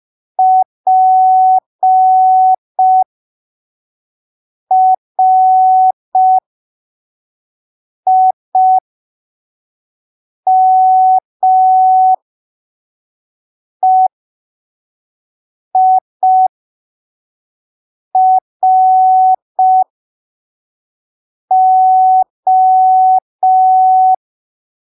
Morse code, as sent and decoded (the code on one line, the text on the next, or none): .--. .-. .. -- . .. .-. ---
PRIMEIRO